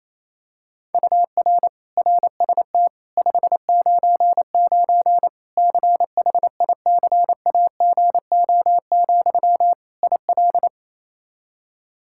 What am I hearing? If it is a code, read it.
Morse code, heard as a keyed tone, about 28 wpm, 700 Hz. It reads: UR RST 599 CHICAGO, IL
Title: Practice QSO - Exchange 1 - Beginning - 28wpm